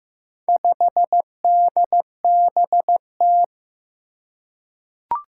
Morse code, heard as a keyed tone, about 15 wpm, 700 Hz. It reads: 5DBT
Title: Four Challenging Letter-Number Combinations 15wpm